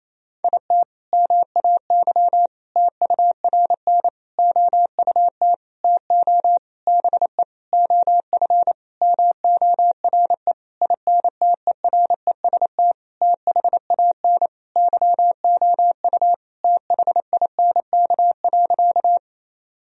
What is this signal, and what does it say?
Morse code, heard as a keyed tone, about 28 wpm, 700 Hz. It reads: IT MAY TURN OUT TO BE OF MORE INTEREST THAN YOU THINK.
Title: Sentences from Top 1000 Words - Review Part 2 - 28wpm